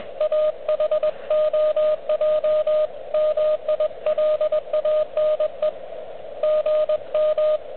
Morse code, heard as a keyed tone, about 19 wpm, 590 Hz, with radio noise. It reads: AHOJ MILANE GM
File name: OK QRP 2022 / Zvuk / REC37